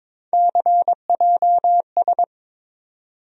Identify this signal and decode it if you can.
Morse code, heard as a keyed tone, about 22 wpm, 700 Hz. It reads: CJS
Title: USA Call Sign Postfixes - 3 Letters 22wpm